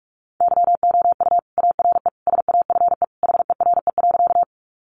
Morse code, heard as a keyed tone, about 44 wpm, 700 Hz. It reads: YOU ARE SAFE HERE.
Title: Sentences from Top 900 Words 45wpm